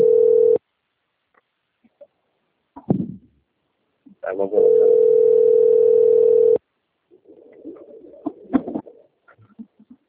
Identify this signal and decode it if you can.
Ringback tone (North America), 440 + 480 Hz, with speech.
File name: out- - - - - .6827